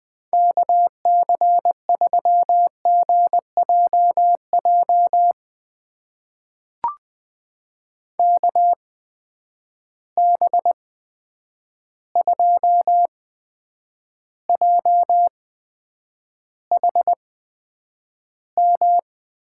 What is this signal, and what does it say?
Morse code, heard as a keyed tone, about 20 wpm, spaced as for 9 wpm, 700 Hz. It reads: KC3GJJ KB2JHM
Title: USA Call Signs - 6 Characters - 8x Character Spacing - 20wpm